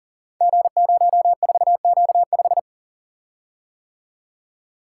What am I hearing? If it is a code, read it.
Morse code, heard as a keyed tone, about 42 wpm, 700 Hz. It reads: G04Q5
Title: Code Groups - 5 Letter:Numbers - 40wpm